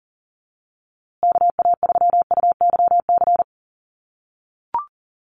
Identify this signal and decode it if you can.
Morse code, heard as a keyed tone, about 40 wpm, 700 Hz. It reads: KA3UYC
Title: USA Call Signs Repeated 2x 40wpm